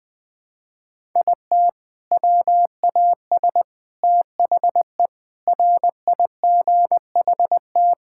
Morse code, heard as a keyed tone, about 21 wpm, 700 Hz. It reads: IT WAS THE RIGHT
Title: Sentences with 6-Word Phrases - Spoken Beginning - 20wpm